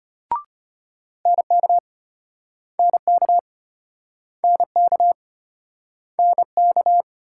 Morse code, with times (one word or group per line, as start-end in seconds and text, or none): 1.3-1.8 NK
2.8-3.4 NK
4.4-5.1 NK
6.2-7.0 NK